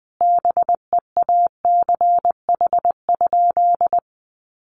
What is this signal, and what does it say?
Morse code, heard as a keyed tone, about 20 wpm, 700 Hz. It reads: BEACH?
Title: Sentences from Top 1500 Words 20wpm